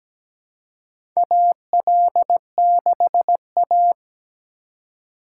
Morse code, heard as a keyed tone, about 17 wpm, 700 Hz. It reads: AL6A
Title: USA Call Signs - 4 Characters - 3x Character Spacing - 17wpm